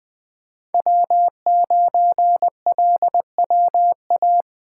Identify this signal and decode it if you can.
Morse code, heard as a keyed tone, about 20 wpm, 700 Hz. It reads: W9LWA
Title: USA Call Signs - 5 Characters - 2x Character Spacing - 20wpm